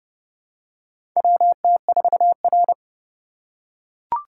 Morse code, heard as a keyed tone, about 32 wpm, 700 Hz. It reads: WT4R
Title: USA Call Signs - 4 Characters - 4x Character Spacing - 30wpm